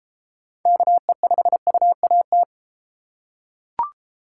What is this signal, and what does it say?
Morse code, heard as a keyed tone, about 30 wpm, 700 Hz. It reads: KE5UAT